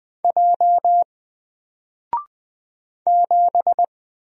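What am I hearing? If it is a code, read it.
Morse code, heard as a keyed tone, about 20 wpm, 700 Hz. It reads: J 7
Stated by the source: Single letter and number Morse drill at 20 wpm, from N4LCD.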